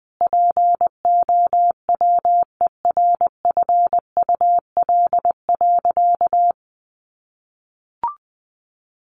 Morse code, heard as a keyed tone, about 20 wpm, 700 Hz. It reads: POWERFUL.
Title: Sentences from Top 1300 Words 20wpm